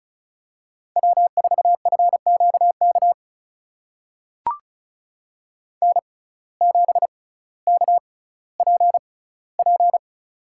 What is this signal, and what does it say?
Morse code, heard as a keyed tone, about 36 wpm, spaced as for 18 wpm, 700 Hz. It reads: W4FQK N7KPP
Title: USA Call Signs - 5 Characters - 6x Character Spacing - 35wpm